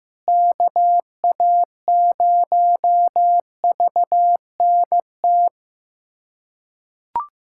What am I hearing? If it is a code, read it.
Morse code, heard as a keyed tone, about 15 wpm, 700 Hz. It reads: KA0VNT
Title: USA Callsign DE Callsign - 6 Characters - 15wpm